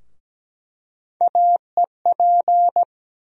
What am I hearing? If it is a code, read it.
Morse code, heard as a keyed tone, about 17 wpm, 700 Hz. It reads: AEP
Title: USA Call Sign Postfixes - 3 Letters 17wpm